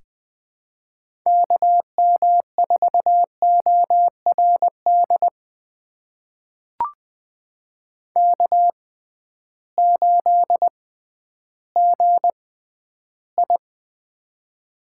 Morse code, heard as a keyed tone, about 20 wpm, spaced as for 12 wpm, 700 Hz. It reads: KM4ORD K8GI
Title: USA Call Signs - 6x Character Spacing - 20wpm